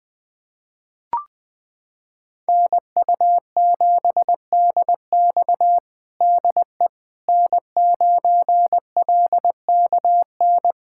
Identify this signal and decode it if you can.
Morse code, heard as a keyed tone, about 20 wpm, 700 Hz. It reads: NU7DX DE N9LKN
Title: USA Callsign DE Callsign - 5 Characters - 20wpm